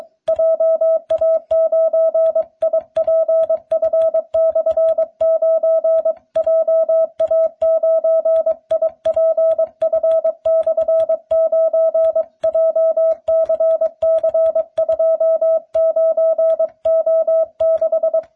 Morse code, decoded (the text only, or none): JA9IPF/9 JA9IPF/9 JCC29O6